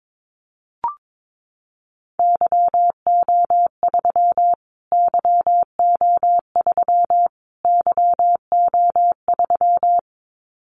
Morse code, heard as a keyed tone, about 22 wpm, 700 Hz. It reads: YO3 YO3 YO3
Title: Callsign Prefixes for Top 25 DXCC - Repeated 3x - 22wpm